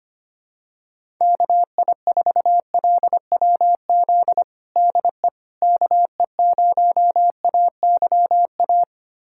Morse code, heard as a keyed tone, about 25 wpm, 700 Hz. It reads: KI4LWZ DE KE0AYA
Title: USA Callsign DE Callsign - 6 Characters - 25wpm